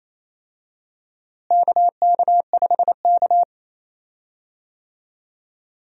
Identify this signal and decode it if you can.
Morse code, heard as a keyed tone, about 28 wpm, 700 Hz. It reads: KK5K
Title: USA Call Signs - 4 Characters - Repeated 2x - 28wpm